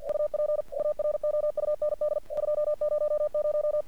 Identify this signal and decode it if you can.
Morse code, heard as a keyed tone, about 46 wpm, 600 Hz. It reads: UP AROUND 200
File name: archived / CRSnet